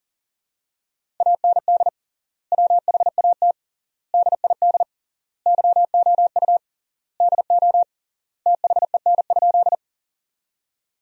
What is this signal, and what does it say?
Morse code, heard as a keyed tone, about 40 wpm, 700 Hz. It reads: AND WHAT DID YOU DO THEN?
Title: 3x Word-Spacing - Sentences from Top 100 Words - 40wpm